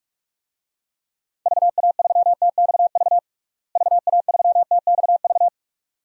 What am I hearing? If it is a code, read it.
Morse code, heard as a keyed tone, about 43 wpm, 700 Hz. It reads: VA3TXV VA3TXV
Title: DXCC-100 Call Signs - 6 Characters - Repeated 2x - 45wpm